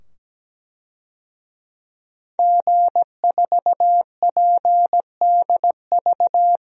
Morse code, heard as a keyed tone, about 17 wpm, 700 Hz. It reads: G4PDV
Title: DXCC-100 Call Signs - 5 Characters - 17wpm